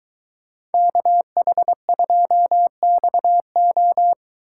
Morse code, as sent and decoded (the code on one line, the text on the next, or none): -.- .... ..--- -..- ---
KH2XO